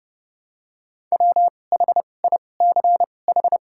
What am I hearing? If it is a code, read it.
Morse code, heard as a keyed tone, about 29 wpm, spaced as for 21 wpm, 700 Hz. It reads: WHICH
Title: 2x Character Spacing - Top 300 Words - 30wpm